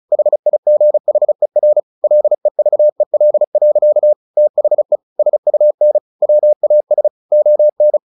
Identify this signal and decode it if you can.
Morse code, heard as a keyed tone, about 34 wpm, 600 Hz. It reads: HIGHER LEVEL. THE SUN WAS ON
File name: the old man and the sea / the old man and the sea 1 35wpm